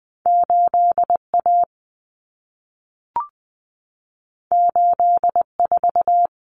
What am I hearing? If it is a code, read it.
Morse code, heard as a keyed tone, about 20 wpm, 700 Hz. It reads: 8A 84
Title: Common Cut Numbers - 2-Digit 20wpm